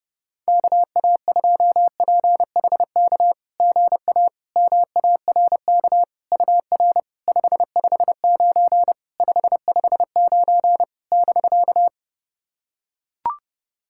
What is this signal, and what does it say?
Morse code, heard as a keyed tone, about 30 wpm, 690 Hz. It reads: KA2PHK GA MARK UR 559 559 <BK>
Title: SOTA QSO Practice 30wpm